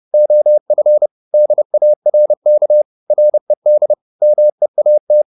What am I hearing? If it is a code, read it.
Morse code, heard as a keyed tone, about 30 wpm, 600 Hz. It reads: OF DARK RED MEAT
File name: the old man and the sea / the old man and the sea 1 30wpm